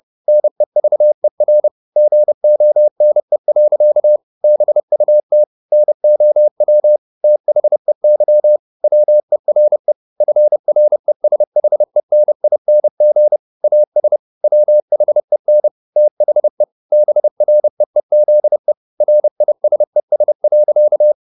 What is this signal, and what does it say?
Morse code, heard as a keyed tone, about 30 wpm, 600 Hz. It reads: NEVER GONE. BUT NOW THEY WERE FRESHENING AS WHEN THE BREEZE RISES.
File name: the old man and the sea / the old man and the sea 1 30wpm